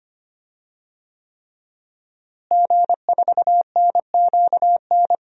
Morse code, heard as a keyed tone, about 25 wpm, 700 Hz. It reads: G4NQN
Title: DXCC-100 Call Signs - 5 Characters - Repeated 3x - 25wpm